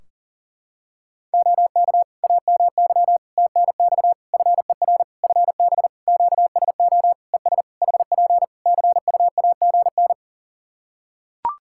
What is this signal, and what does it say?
Morse code, heard as a keyed tone, about 37 wpm, 700 Hz. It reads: OK AMY TNX FER FB QSO ES HP CUAGN
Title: Practice QSO - Exchange 4 - Ending - 40wpm